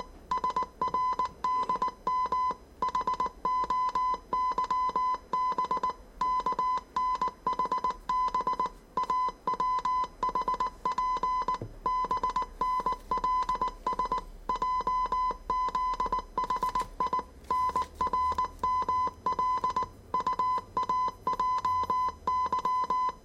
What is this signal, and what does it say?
Morse code, heard as a keyed tone, about 19 wpm, 1010 Hz. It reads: SRDM HOYB KNHB AWHP BNLS JZHI NRML UAJY
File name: 业余无线电 / ●未分类 / 字码80